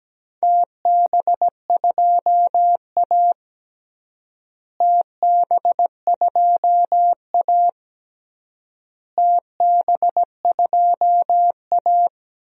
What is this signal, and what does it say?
Morse code, heard as a keyed tone, about 17 wpm, 700 Hz. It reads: TB2A TB2A TB2A
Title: DXCC-100 Call Signs - 4 Characters - Repeated 3x - 17wpm